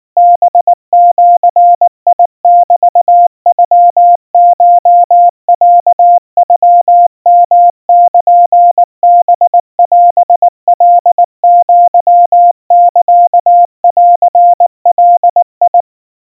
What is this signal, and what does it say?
Morse code (Japanese Wabun code), heard as a keyed tone, about 19 wpm, 700 Hz. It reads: ハジメノコロノヨルハオカアサンガ
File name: kawa / Morse sample / kamyu 70 cpm 700 Hz Crispy